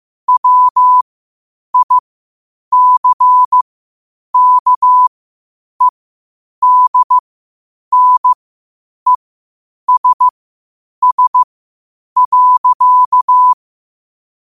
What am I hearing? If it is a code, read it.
Morse code, heard as a keyed tone, about 15 wpm, spaced as for 8 wpm, 1000 Hz. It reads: WICKEDNESS.
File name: qotdfiles / 8wpm